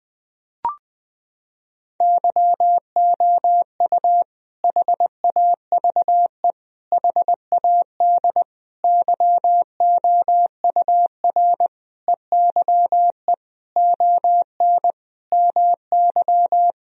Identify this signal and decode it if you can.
Morse code, heard as a keyed tone, about 20 wpm, 700 Hz. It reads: YOU HAVE HAD YOUR EYE ON MY
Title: Sentences from Top 1000 Words - Review Part 1 - 20wpm